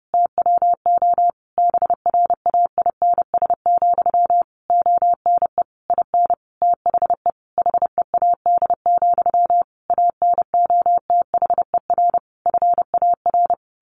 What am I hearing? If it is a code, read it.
Morse code, heard as a keyed tone, about 30 wpm, 700 Hz. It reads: TWO BRAINS, ONE IN THE HEAD, ANOTHER FAR
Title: A Sound Of Thunder 30wpm